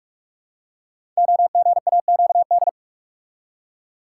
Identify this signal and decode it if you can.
Morse code, heard as a keyed tone, about 44 wpm, 700 Hz. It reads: OGAQD
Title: Code Groups - 5 Letters - 45wpm